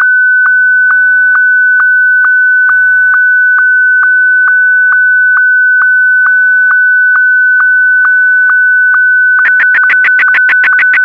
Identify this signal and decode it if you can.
SSTV (slow-scan television), Martin 1.